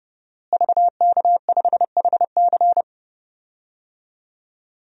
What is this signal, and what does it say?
Morse code, heard as a keyed tone, about 32 wpm, 700 Hz. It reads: VK5HC
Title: DXCC-100 Call Signs - 5 Characters - Repeated 3x - 30wpm